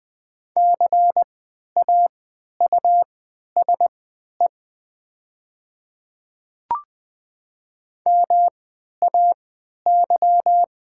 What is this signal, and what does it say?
Morse code, heard as a keyed tone, about 20 wpm, spaced as for 11 wpm, 700 Hz. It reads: CAUSE MAY
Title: 3x Character Spacing - Top 200 Words - 20wpm